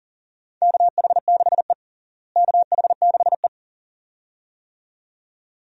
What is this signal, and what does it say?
Morse code, heard as a keyed tone, about 40 wpm, 700 Hz. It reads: KH6E KH6E
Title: DXCC-100 Call Signs - 4 Characters - Repeated 2x - 40wpm